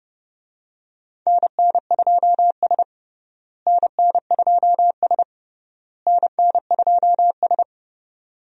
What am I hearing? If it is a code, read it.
Morse code, heard as a keyed tone, about 30 wpm, 700 Hz. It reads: NN2S NN2S NN2S
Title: USA Call Signs - 4 Characters - Repeated 3x - 30wpm